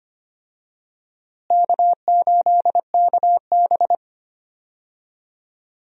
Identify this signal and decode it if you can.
Morse code, heard as a keyed tone, about 25 wpm, 700 Hz. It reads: K8KB
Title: USA Call Signs - 4 Characters - Repeated 3x - 25wpm